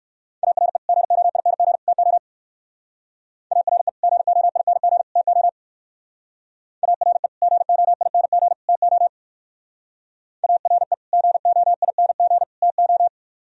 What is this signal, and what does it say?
Morse code, heard as a keyed tone, about 57 wpm, 700 Hz. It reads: ARE GOING TO ARE GOING TO ARE GOING TO ARE GOING TO